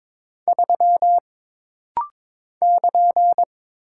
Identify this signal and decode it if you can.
Morse code, heard as a keyed tone, about 22 wpm, 700 Hz. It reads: 3 <KN>
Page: This rapid-fire timing is geared to push your ICR (Instant Character Recognition) skills.